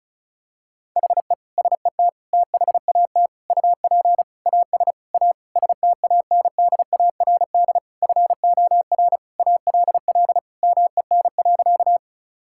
Morse code, heard as a keyed tone, about 36 wpm, 700 Hz. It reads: HE SET THAT UP AS A STANDARD FOR ALL MEN.
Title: Sentences from Top 1000 Words - Review Part 6 - 35wpm